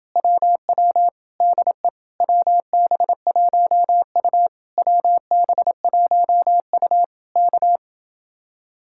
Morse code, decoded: WW DE WB1U WB1U K

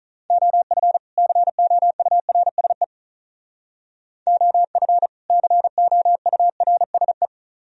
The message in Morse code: OF COURSE OF COURSE

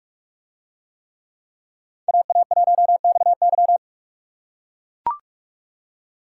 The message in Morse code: AA1XY

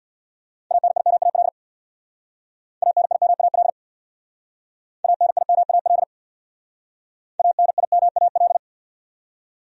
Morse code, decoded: ANIMAL ANIMAL ANIMAL ANIMAL

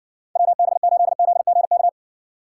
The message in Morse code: WB8ZGK

